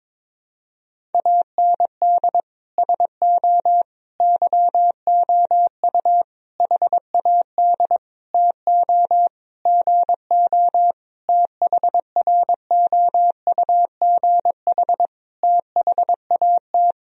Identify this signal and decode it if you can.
Morse code, heard as a keyed tone, about 22 wpm, 700 Hz. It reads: AND SO YOU HAD TO GO THROUGH THAT
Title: Sentences from Top 1000 Words - Review Part 1 - 22wpm